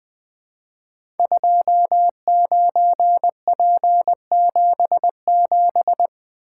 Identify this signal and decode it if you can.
Morse code, heard as a keyed tone, about 20 wpm, 700 Hz. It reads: 29P77
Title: Code Groups - 5 Letter:Numbers - 20wpm